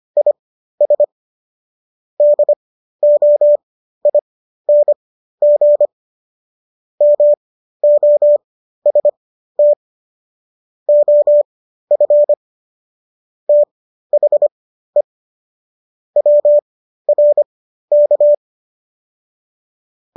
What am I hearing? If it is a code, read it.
Morse code, heard as a keyed tone, about 25 wpm, spaced as for 13 wpm, 590 Hz. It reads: IS DOING MOST OF THE WRK